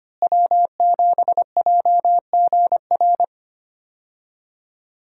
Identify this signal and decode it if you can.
Morse code, heard as a keyed tone, about 25 wpm, 700 Hz. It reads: W7JGR
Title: USA Call Signs - 5 Characters - Repeated 3x - 25wpm